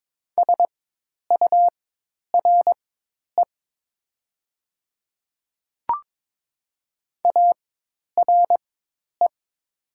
Morse code, heard as a keyed tone, about 22 wpm, spaced as for 10 wpm, 690 Hz. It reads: SURE ARE
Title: 4x Character Spacing - Top 300 Words - 22wpm